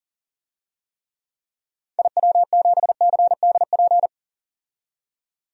Morse code, heard as a keyed tone, about 40 wpm, 700 Hz. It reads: IW7CDP
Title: DXCC-100 Call Signs - 6 Characters - 40wpm